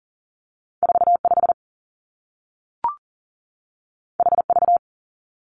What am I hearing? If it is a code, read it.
Morse code, heard as a keyed tone, about 39 wpm, 700 Hz. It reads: 45 HV
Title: Two Challenging Letter-Number Combinations 40wpm